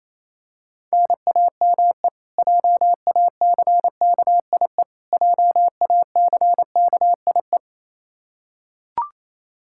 Morse code, heard as a keyed tone, about 28 wpm, 700 Hz. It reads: NAME JACKIE JACKIE
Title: Practice QSO - Exchange 1 - Beginning - 28wpm